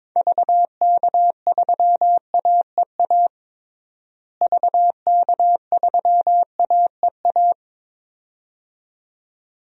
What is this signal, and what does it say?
Morse code, heard as a keyed tone, about 22 wpm, 700 Hz. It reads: VK3AEA VK3AEA
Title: DXCC-100 Call Signs - 6 Characters - Repeated 2x - 22wpm